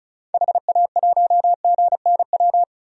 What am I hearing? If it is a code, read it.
Morse code, heard as a keyed tone, about 36 wpm, 700 Hz. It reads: HA1GNW
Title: DXCC-100 Call Signs - 6 Characters - 35wpm